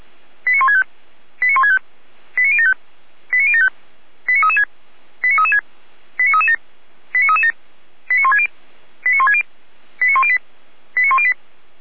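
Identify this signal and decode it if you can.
Five-tone selective call (ZVEI), heard as quick runs of five tones, tones of about 70 ms. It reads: 89165 89165 89075 89075 89336 89336 89337 89337 89160 89160 89108 89108
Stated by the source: ZVEI five-tone selective call 89072 received by SDR.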